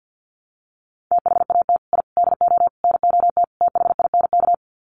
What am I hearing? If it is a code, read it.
Morse code, heard as a keyed tone, about 49 wpm, 700 Hz. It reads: THAT I DO NOT THINK
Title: Sets of 5 Words Encore 50wpm